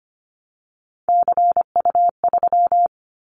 Morse code, heard as a keyed tone, about 25 wpm, 700 Hz. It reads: CU3
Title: Callsign Prefixes for Top 50 DXCC - 25wpm